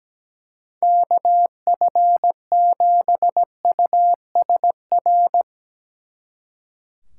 Morse code, heard as a keyed tone, about 19 wpm, 700 Hz. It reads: KF7USR
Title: Speed Racing - USA 2x3 Call Signs 17wpm